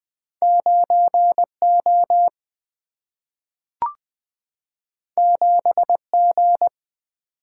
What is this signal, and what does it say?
Morse code, heard as a keyed tone, about 20 wpm, 700 Hz. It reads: 9 O 7 G
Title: Single-Letter Single-Number 20wpm